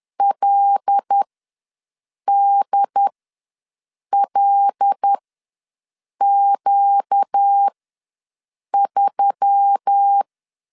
Morse code, heard as a keyed tone, about 10 wpm, spaced as for 6 wpm, 800 Hz. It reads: LDLQ3
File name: LEZIONI / 5 LBVC / 5 L ws 10wpm 3636